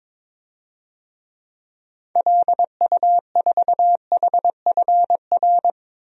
Morse code, heard as a keyed tone, about 22 wpm, 700 Hz. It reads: LU4HFR
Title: DXCC-100 Call Signs - 6 Characters - 22wpm